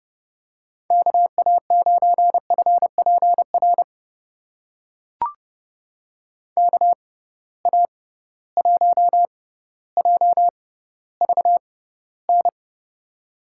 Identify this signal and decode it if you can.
Morse code, heard as a keyed tone, about 32 wpm, spaced as for 16 wpm, 700 Hz. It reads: KA9FPR KA1JVN
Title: USA Call Signs - 6 Characters - 6x Character Spacing - 30wpm